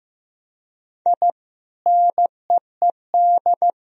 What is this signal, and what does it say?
Morse code, heard as a keyed tone, about 15 wpm, 700 Hz. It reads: I NEED
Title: Sentences from Top 1000 Words - Review Part 1 - 15wpm